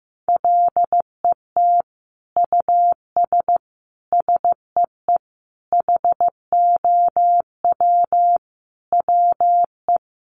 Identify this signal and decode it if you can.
Morse code, heard as a keyed tone, about 15 wpm, 700 Hz. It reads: LET US SEE HOW WE
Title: Sentences from Top 1400 Words 15wpm